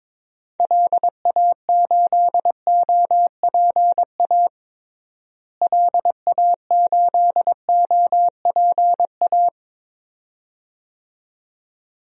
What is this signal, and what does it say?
Morse code, heard as a keyed tone, about 21 wpm, 700 Hz. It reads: LA8OPA LA8OPA
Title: DXCC-100 Call Signs - 6 Characters - Repeated 2x - 22wpm